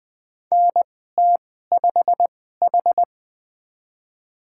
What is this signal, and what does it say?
Morse code, heard as a keyed tone, about 20 wpm, spaced as for 15 wpm, 700 Hz. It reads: NT5H
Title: USA Call Signs - 4 Characters - 2x Character Spacing - 20wpm